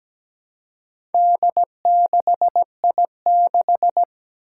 Morse code, heard as a keyed tone, about 17 wpm, 700 Hz. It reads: D6I6
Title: Four Challenging Letter-Number Combinations 17wpm